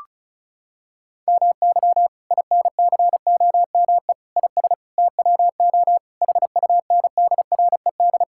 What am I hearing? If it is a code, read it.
Morse code, heard as a keyed tone, about 34 wpm, 700 Hz. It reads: MY INCOME IS TWO HUNDRED
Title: Sentences from Top 1000 Words - Review Part 5 - 35wpm